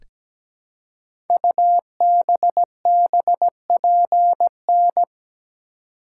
Morse code, heard as a keyed tone, about 17 wpm, 700 Hz. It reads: UBBPN
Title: Code Groups - 5 Letter:Numbers - 17wpm